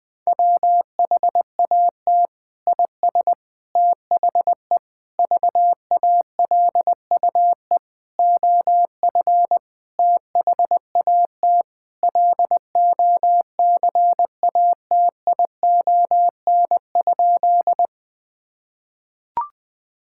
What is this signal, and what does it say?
Morse code, heard as a keyed tone, about 21 wpm, 700 Hz. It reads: WHAT IS THE VALUE OF THAT LOCATION?
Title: Sentences from Top 1000 Words - Review Part 6 - 20wpm